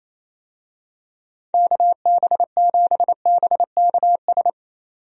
Morse code, heard as a keyed tone, about 28 wpm, 700 Hz. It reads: KB7BKS
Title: USA Call Signs - 6 Characters - 28wpm